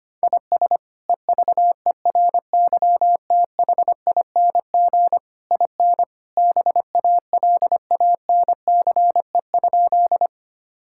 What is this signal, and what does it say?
Morse code, heard as a keyed tone, about 25 wpm, 700 Hz. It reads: IS EVERYTHING IN BALANCE?
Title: Sentences from Top 1000 Words - Review Part 6 - 25wpm